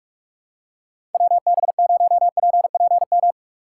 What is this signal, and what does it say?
Morse code, heard as a keyed tone, about 45 wpm, 700 Hz. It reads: WB0PPM